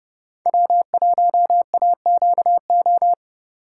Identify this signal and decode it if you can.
Morse code, heard as a keyed tone, about 30 wpm, 700 Hz. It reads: W1AQO